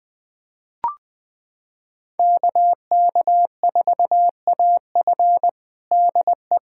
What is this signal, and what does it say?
Morse code, heard as a keyed tone, about 20 wpm, 700 Hz. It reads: KK4AF DE
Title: USA Callsign DE Callsign - 5 Characters - 20wpm